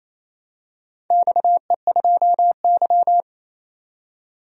Morse code, heard as a keyed tone, about 27 wpm, 700 Hz. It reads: XE2Y
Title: DXCC-100 Call Signs - 4 Characters - 28wpm